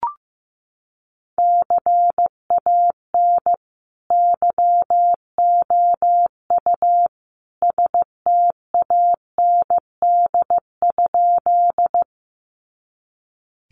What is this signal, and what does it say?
Morse code, heard as a keyed tone, about 15 wpm, 700 Hz. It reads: CAN YOU STAND?
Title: Sentences from Top 300 Words 15wpm